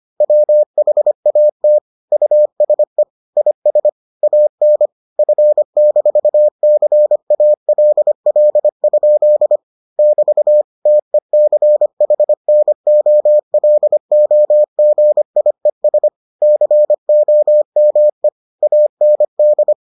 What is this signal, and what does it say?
Morse code, heard as a keyed tone, about 25 wpm, 600 Hz. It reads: WHAT USE IS AN F-CALL? <BT> TECHNOLOGIES COME AND